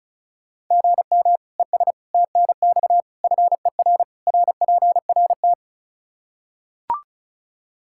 Morse code, heard as a keyed tone, about 36 wpm, 700 Hz. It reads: GM ES TNX FER RPRT
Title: K1USN SST QSO Practice 35wpm